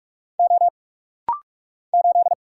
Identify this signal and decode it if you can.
Morse code, heard as a keyed tone, about 46 wpm, 700 Hz. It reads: O 8
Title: Single Letter-Number - Rapid-Fire 45wpm